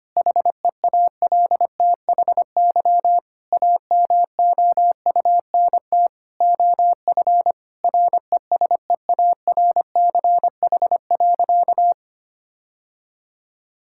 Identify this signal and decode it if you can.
Morse code, heard as a keyed tone, about 25 wpm, 700 Hz. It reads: HEALTHY AMOUNT OF RESEARCH.
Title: Sentences from Top 1000 Words - Review Part 5 - 25wpm